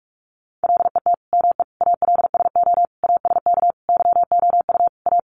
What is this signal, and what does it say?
Morse code, heard as a keyed tone, about 44 wpm, 700 Hz. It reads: LET ME ALSO ASK YOU A
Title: Sentences from Top 400 Words 45wpm